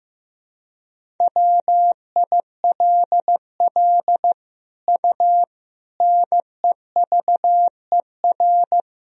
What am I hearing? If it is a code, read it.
Morse code, heard as a keyed tone, about 16 wpm, 700 Hz. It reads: WILL U NEVER
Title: Sentences from Top 300 Words and Abbreviations 15wpm